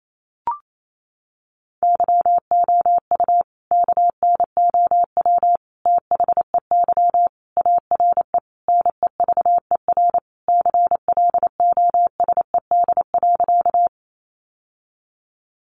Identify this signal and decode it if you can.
Morse code, heard as a keyed tone, about 28 wpm, 700 Hz. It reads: YOU KNOW THEY ARE NEVER CLOSED.
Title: Sentences from Top 1400 Words 28wpm